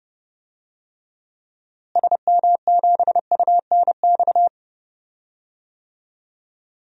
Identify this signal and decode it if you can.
Morse code, heard as a keyed tone, about 32 wpm, 700 Hz. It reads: SM7UNX
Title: DXCC-100 Call Signs - 6 Characters - Repeated 3x - 30wpm